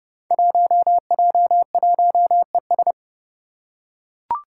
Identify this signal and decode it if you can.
Morse code, heard as a keyed tone, about 30 wpm, 700 Hz. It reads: 1J1ES
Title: Five Challenging Letter-Number Combinations 30wpm